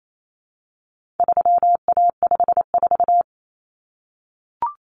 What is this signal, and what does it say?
Morse code, heard as a keyed tone, about 28 wpm, 700 Hz. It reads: 3A54
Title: Common Cut Numbers - 4-Digit 28wpm